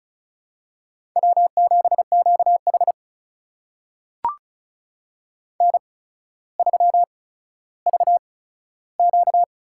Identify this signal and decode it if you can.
Morse code, heard as a keyed tone, about 36 wpm, spaced as for 15 wpm, 700 Hz. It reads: W7QH N3VQ
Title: USA Call Signs - 4 Characters - 8x Character Spacing - 35wpm